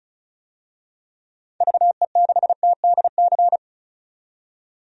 Morse code, heard as a keyed tone, about 36 wpm, 700 Hz. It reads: VE6TDC
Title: DXCC-100 Call Signs - 6 Characters - 35wpm